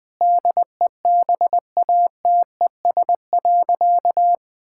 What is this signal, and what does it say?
Morse code, heard as a keyed tone, about 20 wpm, 700 Hz. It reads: DEBATES.
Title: Sentences with 6-Word Phrases - 20wpm